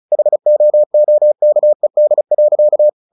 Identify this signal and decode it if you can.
Morse code, heard as a keyed tone, about 34 wpm, 600 Hz. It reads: HOOKED.
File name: the old man and the sea / the old man and the sea 1 35wpm